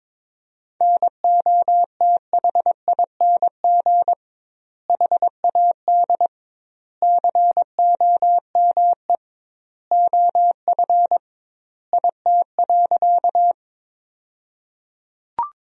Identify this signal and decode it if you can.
Morse code, heard as a keyed tone, about 21 wpm, 700 Hz. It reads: NOTHING HAD COME OF IT.
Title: 2x Word-Spacing - Sentences from Top 500 Words - 22wpm